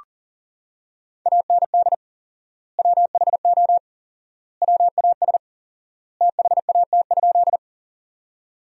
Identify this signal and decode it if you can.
Morse code, heard as a keyed tone, about 40 wpm, 700 Hz. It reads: AND WHO WAS THAT?
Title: 4x Word-Spacing - Sentences from Top 100 Words - 40wpm